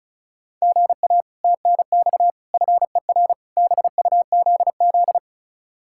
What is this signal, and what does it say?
Morse code, heard as a keyed tone, about 36 wpm, 700 Hz. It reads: GA TNX FER BUZZ
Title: Speed-Racing Common QSO Phrases - 30wpm